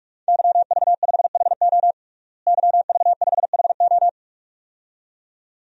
Morse code, heard as a keyed tone, about 47 wpm, 700 Hz. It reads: YV5HO YV5HO
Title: DXCC-100 Call Signs - 5 Characters - Repeated 2x - 45wpm